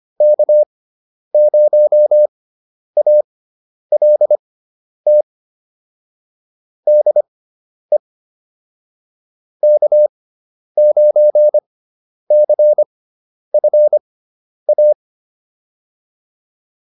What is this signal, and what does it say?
Morse code, heard as a keyed tone, about 25 wpm, spaced as for 10 wpm, 600 Hz. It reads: K0ALT DE K9CFA